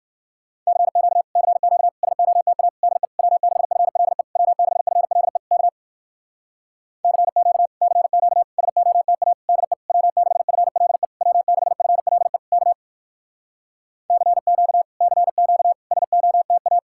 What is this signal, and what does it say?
Morse code, heard as a keyed tone, about 59 wpm, 700 Hz. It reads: CQ CQ SOTA DE W6FLE W6FLE K CQ CQ SOTA DE W6FLE W6FLE K CQ CQ SOTA